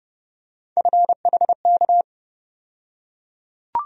Morse code, heard as a keyed tone, about 30 wpm, 700 Hz. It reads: FHK
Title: USA Call Sign Postfixes - 3 Letters 30wpm